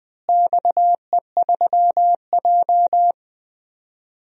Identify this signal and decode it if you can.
Morse code, heard as a keyed tone, about 20 wpm, 700 Hz. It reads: XE3J
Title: DXCC-100 Call Signs - 4 Characters - Repeated 2x - 20wpm